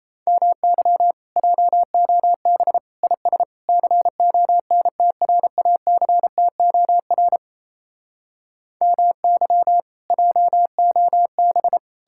Morse code, heard as a keyed tone, about 36 wpm, 700 Hz. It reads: MY JOB IS CONTRACTOR MY JOB